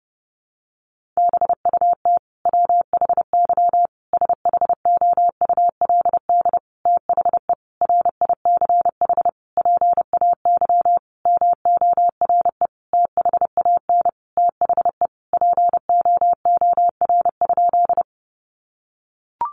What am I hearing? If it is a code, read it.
Morse code, heard as a keyed tone, about 30 wpm, 700 Hz. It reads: BUT WHY SHOULD THE RICH PAY MORE THAN THE POOR?
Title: Sentences from Top 1100 Words 30wpm